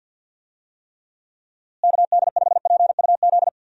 Morse code, heard as a keyed tone, about 51 wpm, 700 Hz. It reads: KD5PUZ
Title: USA Call Signs - 6 Characters - Repeated 2x - 50wpm